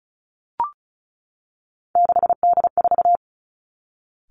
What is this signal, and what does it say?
Morse code, heard as a keyed tone, about 36 wpm, 700 Hz. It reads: 6D4
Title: Three Challenging Letter-Number Combinations 35wpm